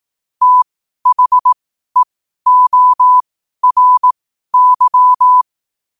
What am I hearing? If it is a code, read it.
Morse code, heard as a keyed tone, about 18 wpm, spaced as for 12 wpm, 1000 Hz. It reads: THEORY
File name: qotdfiles / 12wpm